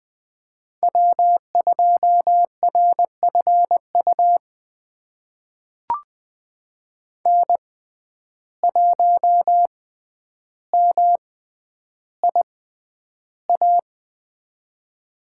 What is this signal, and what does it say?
Morse code, heard as a keyed tone, about 20 wpm, spaced as for 10 wpm, 700 Hz. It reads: W2RFU N1MIA
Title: USA Call Signs - 5 Characters - 6x Character Spacing - 20wpm